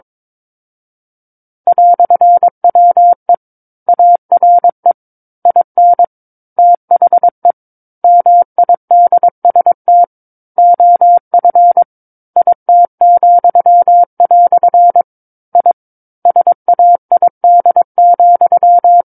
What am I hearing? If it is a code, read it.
Morse code, heard as a keyed tone, about 22 wpm, 700 Hz. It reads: "WE ARE IN THE MIDST OF IT," I SAID,
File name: Book One / 13-Chapter 13 - 22 WPM